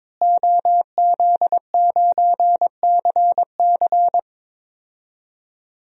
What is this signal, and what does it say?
Morse code, heard as a keyed tone, about 22 wpm, 700 Hz. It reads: OZ9CC
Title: DXCC-100 Call Signs - 5 Characters - Repeated 2x - 22wpm